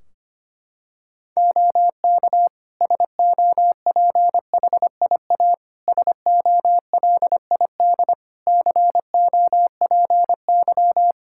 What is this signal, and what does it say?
Morse code, heard as a keyed tone, about 25 wpm, 700 Hz. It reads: OK SOPHIA SOLID COPY
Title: Practice QSO - Exchange 3 - More Info - 25wpm